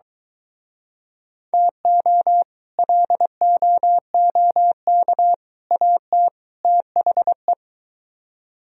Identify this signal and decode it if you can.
Morse code, heard as a keyed tone, about 23 wpm, 700 Hz. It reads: TO LOOK AT THE